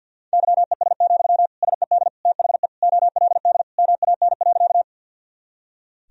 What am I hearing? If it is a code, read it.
Morse code, heard as a keyed tone, about 53 wpm, 700 Hz. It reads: YES, SED THE OLD MAN.
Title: Sentences from Top 500 Words and Abbreviations 50wpm